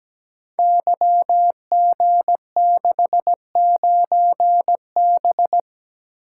Morse code, heard as a keyed tone, about 17 wpm, 700 Hz. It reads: YG69B
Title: Code Groups - 5 Letter:Numbers - 17wpm